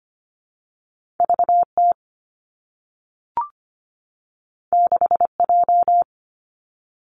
Morse code, heard as a keyed tone, about 25 wpm, 700 Hz. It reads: VT 6J
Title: Two Challenging Letter-Number Combinations 25wpm